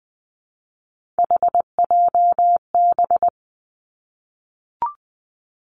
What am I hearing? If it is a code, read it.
Morse code, heard as a keyed tone, about 19 wpm, 700 Hz. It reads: HJB
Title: Three Challenging Letter-Number Combinations 20wpm